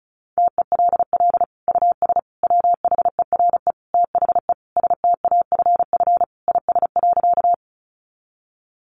Morse code, heard as a keyed tone, about 34 wpm, 700 Hz. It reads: TELL US WHERE THE STAFF IS.
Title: Sentences from Top 1200 Words 35wpm